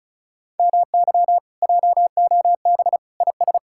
Morse code, heard as a keyed tone, about 44 wpm, 700 Hz. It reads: MY JOB IS